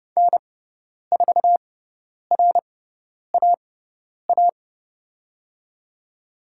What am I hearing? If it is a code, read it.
Morse code, heard as a keyed tone, about 30 wpm, spaced as for 10 wpm, 700 Hz. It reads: N4RAA
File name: Lesson 16g - Letter c - Call Signs 30wpm